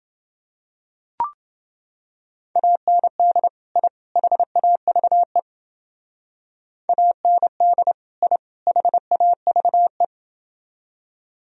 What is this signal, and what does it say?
Morse code, heard as a keyed tone, about 28 wpm, 700 Hz. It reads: AND I HAVE AND I HAVE